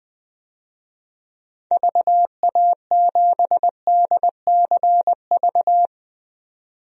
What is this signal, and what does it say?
Morse code, heard as a keyed tone, about 20 wpm, 700 Hz. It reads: VA7DCV
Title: DXCC-100 Call Signs - 6 Characters - 20wpm